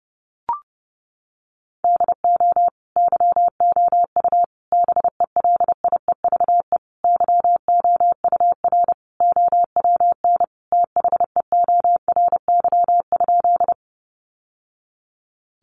Morse code, heard as a keyed tone, about 30 wpm, 700 Hz. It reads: DO YOU BELIEVE YOUR OWN THEORY?
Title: Sentences from Top 1000 Words 30wpm